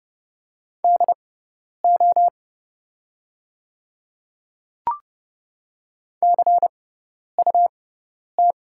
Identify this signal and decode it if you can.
Morse code, heard as a keyed tone, about 30 wpm, spaced as for 10 wpm, 700 Hz. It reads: DO CUT
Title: 6x Character Spacing - Top 300 Words - 30wpm